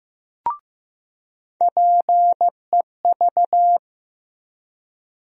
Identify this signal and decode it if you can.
Morse code, heard as a keyed tone, about 15 wpm, 700 Hz. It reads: PEV
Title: USA Call Sign Postfixes - 3 Letters 15wpm